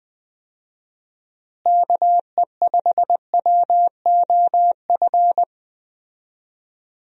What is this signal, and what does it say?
Morse code, heard as a keyed tone, about 20 wpm, 700 Hz. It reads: KE5WOF
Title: USA Call Signs - 6 Characters - Repeated 3x - 20wpm